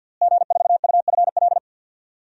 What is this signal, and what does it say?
Morse code, heard as a keyed tone, about 47 wpm, 700 Hz. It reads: G4UFL
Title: DXCC-100 Call Signs - 5 Characters - Repeated 3x - 50wpm